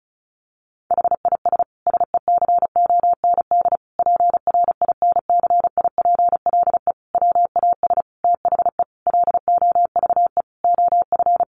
Morse code, heard as a keyed tone, about 34 wpm, 700 Hz. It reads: HIS SECOND PRINCIPLE WAS THE LOVE OF
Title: Sentences from Top 1400 Words 35wpm